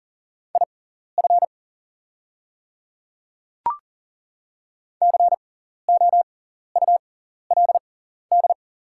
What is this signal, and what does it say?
Morse code, heard as a keyed tone, about 38 wpm, spaced as for 14 wpm, 690 Hz. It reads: IF COULD
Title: 6x Character Spacing - Top 100 Words - 40wpm